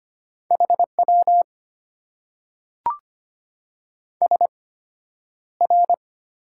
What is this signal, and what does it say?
Morse code, heard as a keyed tone, about 25 wpm, spaced as for 11 wpm, 690 Hz. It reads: HW SR
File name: Top 90 QSO Elements - 8x Character Spacing - 25wpm